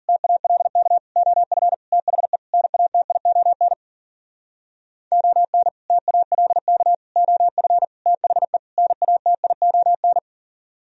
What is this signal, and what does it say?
Morse code, heard as a keyed tone, about 51 wpm, 700 Hz. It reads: TALK OF THE NATION ON TALK OF THE NATION